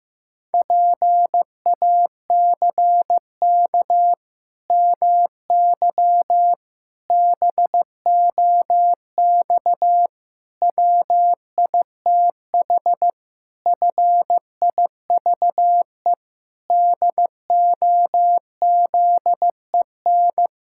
Morse code, heard as a keyed tone, about 15 wpm, 700 Hz. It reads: PACK MY BOX WITH FIVE DOZEN
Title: Pangram Challenge v2 - 15wpm